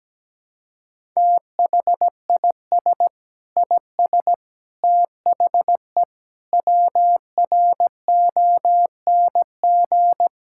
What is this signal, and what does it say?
Morse code, heard as a keyed tone, about 17 wpm, 700 Hz. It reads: THIS IS THE WRONG
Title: Sentences from Top 1000 Words - Review Part 6 - 17wpm